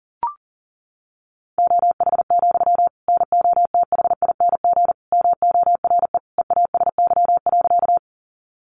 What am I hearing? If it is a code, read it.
Morse code, heard as a keyed tone, about 40 wpm, 700 Hz. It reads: OH, NOTHING MORE EASY.
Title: Sentences from Top 700 Words 40wpm